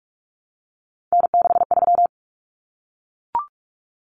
Morse code, heard as a keyed tone, about 44 wpm, 700 Hz. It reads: N63
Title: Common Cut Numbers - 3-Digit 45wpm